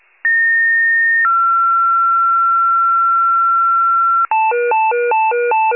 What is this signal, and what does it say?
Two-tone page, heard as a one-second tone, then a three-second tone, about 1850 then 1360 Hz. It is heard from 0.2 to 4.3 s.